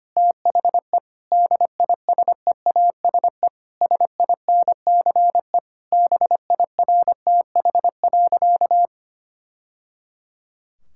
Morse code, heard as a keyed tone, about 25 wpm, 700 Hz. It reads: THE DISEASE SINCE BIRTH.
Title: Sentences from Top 1000 Words - Review Part 6 - 25wpm